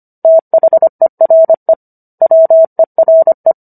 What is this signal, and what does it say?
Morse code, heard as a keyed tone, about 25 wpm, 650 Hz. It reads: THERE WERE